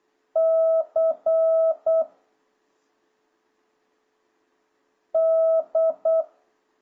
Morse code, heard as a keyed tone, about 8 wpm, 660 Hz. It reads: C D